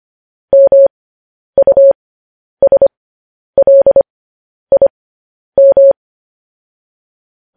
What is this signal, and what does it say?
Morse code, heard as a keyed tone, about 25 wpm, spaced as for 10 wpm, 550 Hz. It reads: MUSLIM